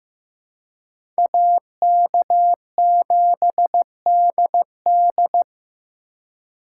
Morse code, heard as a keyed tone, about 18 wpm, 700 Hz. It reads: AK7DD
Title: Speed Racing - USA 2x2 Call Signs 15wpm